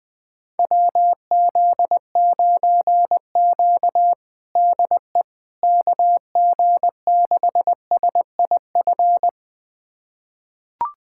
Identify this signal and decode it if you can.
Morse code, heard as a keyed tone, about 20 wpm, 700 Hz. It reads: WZ9Q DE KG6SIF
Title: USA Callsign DE Callsign 20wpm